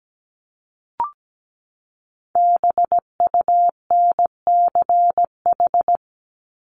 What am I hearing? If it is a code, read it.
Morse code, heard as a keyed tone, about 17 wpm, 700 Hz. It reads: BUNCH
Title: Top 1751-2000 Words 17wpm